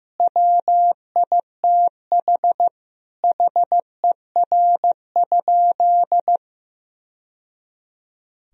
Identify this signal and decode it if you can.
Morse code, heard as a keyed tone, about 14 wpm, 700 Hz. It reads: WITH HER?
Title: Sentences from Top 1000 Words - Review Part 3 - 15wpm